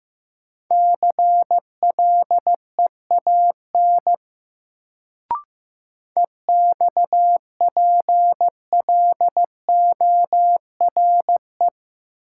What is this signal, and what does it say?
Morse code, heard as a keyed tone, about 15 wpm, 700 Hz. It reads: CLEAN EXPLORE
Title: Sentences from Top 1100 Words - New Word Drill Alt - 15wpm